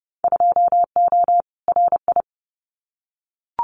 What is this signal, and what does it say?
Morse code, heard as a keyed tone, about 30 wpm, 700 Hz. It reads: 2O RI
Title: Winter Field Day Exchanges 30wpm